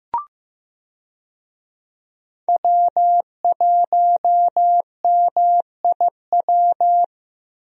Morse code, heard as a keyed tone, about 15 wpm, 700 Hz. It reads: W1MIW
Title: USA Call Signs - 5 Characters - Repeated 3x - 15wpm